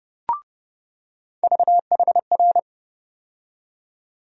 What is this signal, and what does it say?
Morse code, heard as a keyed tone, about 30 wpm, 700 Hz. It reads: VHR
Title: USA Call Sign Postfixes - 3 Letters 30wpm